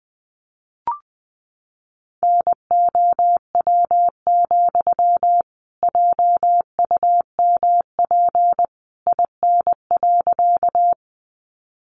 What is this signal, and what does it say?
Morse code, heard as a keyed tone, about 20 wpm, 700 Hz. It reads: NOW, JUMP IN.
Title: Sentences from Top 1300 Words 20wpm